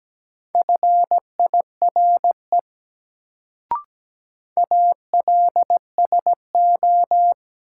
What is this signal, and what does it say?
Morse code, heard as a keyed tone, about 17 wpm, 700 Hz. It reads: FIRE ALSO
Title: Sentences from Top 400 Words - New Word Drill - 17wpm